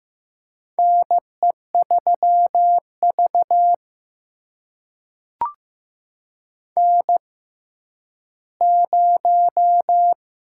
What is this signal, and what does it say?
Morse code, heard as a keyed tone, about 15 wpm, spaced as for 10 wpm, 700 Hz. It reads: NE3V N0
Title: USA Call Signs - 4 Characters - 6x Character Spacing - 15wpm